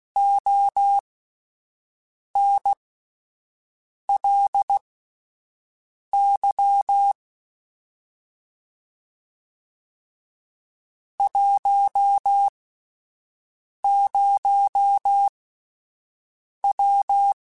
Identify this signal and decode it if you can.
Morse code, heard as a keyed tone, about 16 wpm, spaced as for 5 wpm, 800 Hz. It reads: ONLY 10W
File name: (spaced) / QSOs Spaced-mp3s / WA7QOX 2 @10 sp